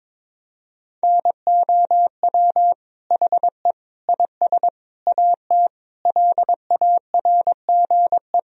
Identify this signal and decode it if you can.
Morse code, heard as a keyed tone, about 22 wpm, 700 Hz. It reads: NOW HE IS AT LARGE
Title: Sentences from Top 1000 Words - Review Part 1 - 22wpm